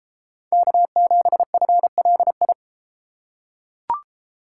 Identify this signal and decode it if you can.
Morse code, heard as a keyed tone, about 31 wpm, 700 Hz. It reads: K7FLI